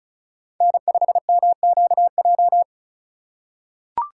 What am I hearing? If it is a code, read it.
Morse code, heard as a keyed tone, about 34 wpm, 700 Hz. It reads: N5MQJ
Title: USA Call Signs - 2x Character Spacing - 35wpm